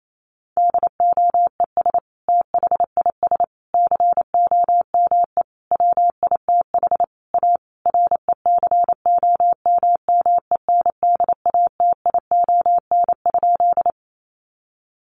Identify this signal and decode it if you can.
Morse code, heard as a keyed tone, about 28 wpm, 700 Hz. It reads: DOES THIS COME WITH A RECOMMENDATION?
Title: Sentences from Top 1400 Words 28wpm